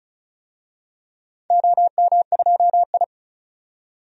Morse code, heard as a keyed tone, about 36 wpm, 700 Hz. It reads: OM2I
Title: DXCC-100 Call Signs - 4 Characters - 35wpm